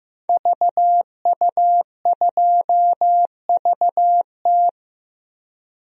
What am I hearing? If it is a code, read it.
Morse code, heard as a keyed tone, about 15 wpm, 700 Hz. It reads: VU2VT
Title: DXCC-100 Call Signs - 5 Characters - 15wpm